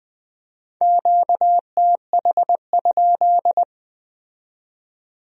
Morse code, heard as a keyed tone, about 20 wpm, 700 Hz. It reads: QTH?
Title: Speed-Racing Q-Codes - 17wpm